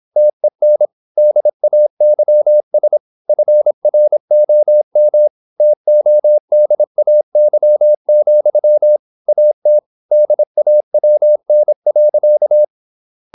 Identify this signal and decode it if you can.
Morse code, heard as a keyed tone, about 26 wpm, 600 Hz. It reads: TEN DAYS FROM TODAY, AT DAWN.
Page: Never 26wpm | CW med Gnister